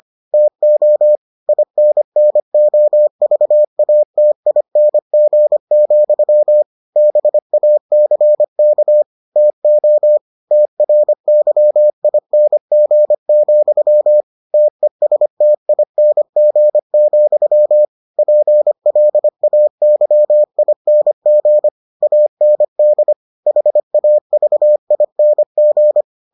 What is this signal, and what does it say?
Morse code, heard as a keyed tone, about 25 wpm, 600 Hz. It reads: TO INNOVATING, BACK TO TRYING, TESTING, PLAYING AND HAVING